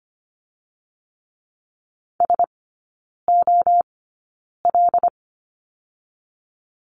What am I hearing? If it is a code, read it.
Morse code, heard as a keyed tone, about 25 wpm, 700 Hz. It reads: S O L
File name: Lesson 17e - Letter u - Three Characters 25wpm